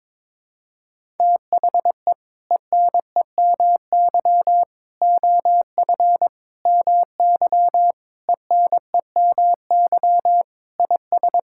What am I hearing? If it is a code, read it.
Morse code, heard as a keyed tone, about 22 wpm, 700 Hz. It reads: THE ENEMY OF MY ENEMY IS
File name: practice / 22wpm / Sets of 8 Words - 22wpm